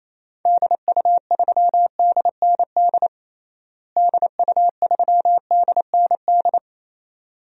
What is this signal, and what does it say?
Morse code, heard as a keyed tone, about 28 wpm, 700 Hz. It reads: DU3DND DU3DND
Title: DXCC-100 Call Signs - 6 Characters - Repeated 2x - 28wpm